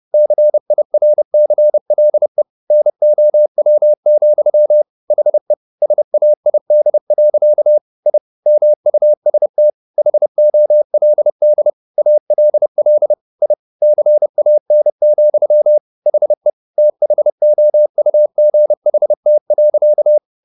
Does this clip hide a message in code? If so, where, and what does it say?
Yes: Morse code from 0.1 to 20.2 s, heard as a keyed tone, about 30 wpm, 600 Hz. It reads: CIRCLE NOW, HE SAID. I MUST HOLD ALL I CAN, HE THOUGHT.